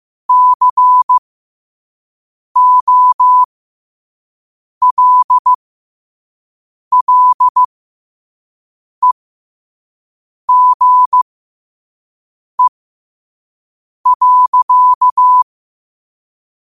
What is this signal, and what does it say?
Morse code, heard as a keyed tone, about 15 wpm, spaced as for 5 wpm, 1000 Hz. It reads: COLLEGE.
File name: qotdfiles / 5wpm